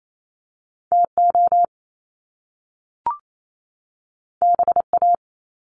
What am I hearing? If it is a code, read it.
Morse code, heard as a keyed tone, about 28 wpm, 700 Hz. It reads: TO BA
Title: Top Two Letter Combinations in Words 28wpm